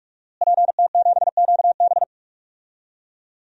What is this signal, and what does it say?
Morse code, heard as a keyed tone, about 46 wpm, 700 Hz. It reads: PT7QB
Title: DXCC-100 Call Signs - 5 Characters - Repeated 3x - 45wpm